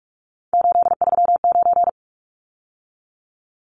Morse code, heard as a keyed tone, about 44 wpm, 700 Hz. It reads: 839
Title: Numbers - 3-Digit 45wpm